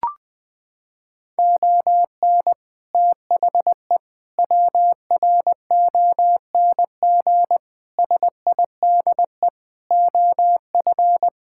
Morse code, heard as a keyed tone, about 20 wpm, 700 Hz. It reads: ON THE WRONG SIDE OF
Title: Sets of 6 Words - 20wpm